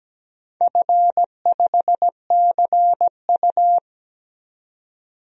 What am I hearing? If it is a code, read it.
Morse code, heard as a keyed tone, about 17 wpm, 700 Hz. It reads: F5CU
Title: DXCC-100 Call Signs - 4 Characters - 17wpm